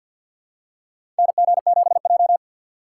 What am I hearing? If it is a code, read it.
Morse code, heard as a keyed tone, about 50 wpm, 700 Hz. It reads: NG7J
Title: USA Call Signs - 4 Characters - 6x Character Spacing - 50wpm